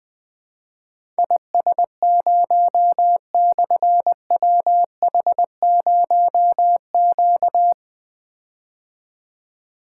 Morse code, heard as a keyed tone, about 20 wpm, 700 Hz. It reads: IS0/WH0Q
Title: Difficult Call Signs 20wpm